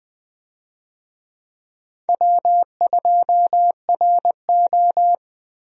Morse code, heard as a keyed tone, about 21 wpm, 700 Hz. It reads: W2RO
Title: USA Call Signs - 4 Characters - Repeated 3x - 20wpm